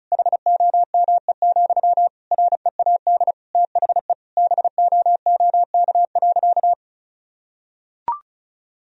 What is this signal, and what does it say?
Morse code, heard as a keyed tone, about 34 wpm, 700 Hz. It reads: HOME, READ THE BOOK.
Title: Sentences from Top 1000 Words - Review Part 2 - 35wpm